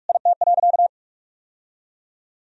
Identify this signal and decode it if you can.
Morse code, heard as a keyed tone, about 44 wpm, 700 Hz. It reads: IT.
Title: Sentences from Top 1000 Words - Review Part 3 - 45wpm